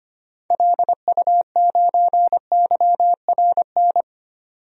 Morse code, heard as a keyed tone, about 25 wpm, 700 Hz. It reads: LU9YRN
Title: DXCC-100 Call Signs - 6 Characters - 25wpm